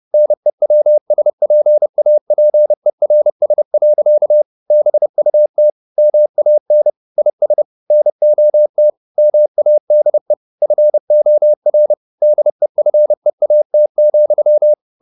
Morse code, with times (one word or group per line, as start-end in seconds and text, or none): 0.1-4.4 NEWSPAPERS.
4.7-5.7 BUT
6.0-6.9 MAN
7.2-7.6 IS
7.9-8.9 NOT
9.2-10.3 MADE
10.6-11.9 FOR
12.2-14.7 DEFEAT,